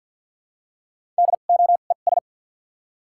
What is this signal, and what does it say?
Morse code, heard as a keyed tone, about 53 wpm, spaced as for 37 wpm, 700 Hz. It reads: DOES
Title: 2x Character Spacing - Top 200 Words - 50wpm